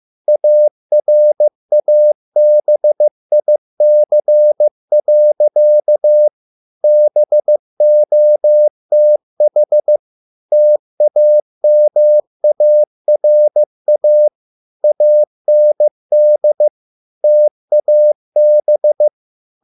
Morse code, heard as a keyed tone, about 15 wpm, 600 Hz. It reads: ARABIC. BOTH TAMARA AND TAB